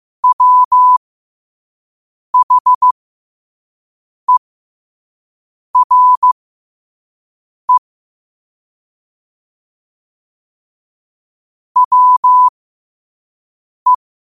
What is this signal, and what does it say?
Morse code, heard as a keyed tone, about 15 wpm, spaced as for 5 wpm, 1000 Hz. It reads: WHERE WE
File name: qotdfiles / 5wpm